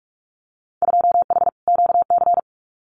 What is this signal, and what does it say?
Morse code, heard as a keyed tone, about 44 wpm, 700 Hz. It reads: 2H QC
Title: Winter Field Day Exchanges 45wpm